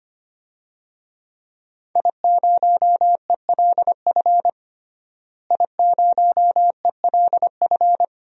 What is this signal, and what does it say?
Morse code, heard as a keyed tone, about 25 wpm, 700 Hz. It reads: I0ELF I0ELF
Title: DXCC-100 Call Signs - 5 Characters - Repeated 2x - 25wpm